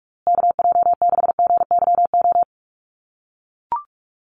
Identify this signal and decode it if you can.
Morse code, heard as a keyed tone, about 44 wpm, 690 Hz. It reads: KJ6GYO
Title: USA Call Signs Repeated 2x 45wpm